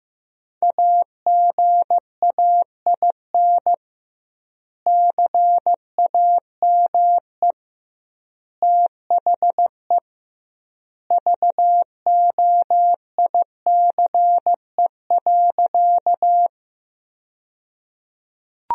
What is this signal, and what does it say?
Morse code, heard as a keyed tone, about 16 wpm, 700 Hz. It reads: AGAIN CAME THE VOICE.
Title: 2x Word-Spacing - Sentences from Top 500 Words - 15wpm